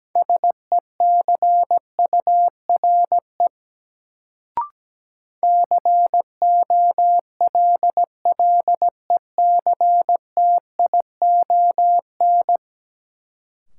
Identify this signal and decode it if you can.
Morse code, heard as a keyed tone, about 17 wpm, 700 Hz. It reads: SECURE COLLECTION
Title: Sentences from Top 1200 Words - New Word Drill - 17wpm